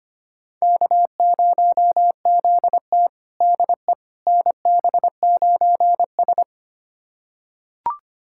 Morse code, heard as a keyed tone, about 25 wpm, 700 Hz. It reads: K0ZT DE NB9S
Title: USA Callsign DE Callsign - 4 Characters - 25wpm